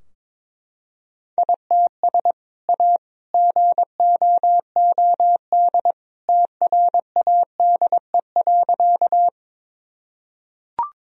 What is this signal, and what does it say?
Morse code, heard as a keyed tone, about 22 wpm, 700 Hz. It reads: ITS A GOOD TRADE.
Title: Sentences from Top 1000 Words - Review Part 4 - 22wpm